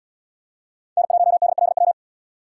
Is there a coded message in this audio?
Yes: Morse code from 1.0 to 1.9 s, heard as a keyed tone, about 72 wpm, 700 Hz. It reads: N1DLW